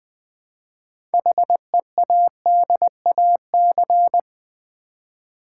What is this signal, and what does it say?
Morse code, heard as a keyed tone, about 20 wpm, 700 Hz. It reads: HEADAC
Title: Build A Word - 2 Letters At A Time - 8-Letter Words - 20wpm